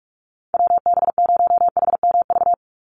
Morse code, heard as a keyed tone, about 44 wpm, 700 Hz. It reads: WB0HMV
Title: USA Call Signs 45wpm